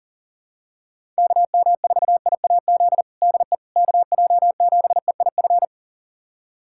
Morse code, heard as a keyed tone, about 40 wpm, 700 Hz. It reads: KM4IAZ DE KJ7EIF
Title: USA Callsign DE Callsign 40wpm